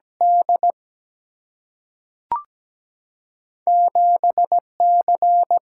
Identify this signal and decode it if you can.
Morse code, heard as a keyed tone, about 17 wpm, 700 Hz. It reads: D 7C
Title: Single-Letter Single-Number 17wpm